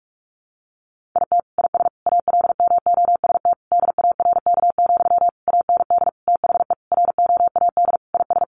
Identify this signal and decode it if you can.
Morse code, heard as a keyed tone, about 44 wpm, 700 Hz. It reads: IT IS ALMOST DARK, AND THE ROAD IS
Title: Sentences from Top 1200 Words 45wpm